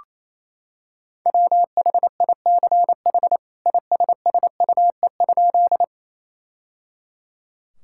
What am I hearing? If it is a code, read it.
Morse code, heard as a keyed tone, about 27 wpm, 700 Hz. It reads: WHICH ISSUE?
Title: Sentences from Top 1000 Words - Review Part 6 - 28wpm